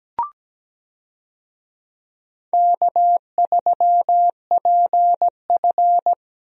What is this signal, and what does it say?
Morse code, heard as a keyed tone, about 17 wpm, 700 Hz. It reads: K3PF
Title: USA Call Signs - 4 Characters - Repeated 2x - 17wpm